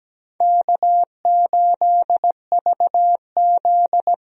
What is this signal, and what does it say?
Morse code, heard as a keyed tone, about 17 wpm, 700 Hz. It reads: K8VZ
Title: USA Call Signs - 4 Characters - 17wpm